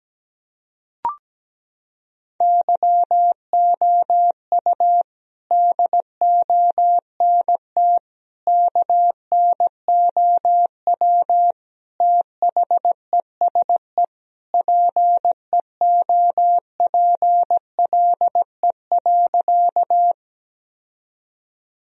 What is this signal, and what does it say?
Morse code, heard as a keyed tone, about 17 wpm, 700 Hz. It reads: YOU DONT KNOW THESE PEOPLE.
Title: Sentences from Top 1000 Words - Review Part 1 - 17wpm